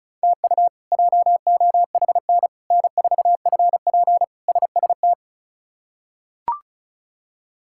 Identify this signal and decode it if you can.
Morse code, heard as a keyed tone, about 34 wpm, 700 Hz. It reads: TU JOHN N4FP SST
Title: K1USN SST QSO Practice - 35wpm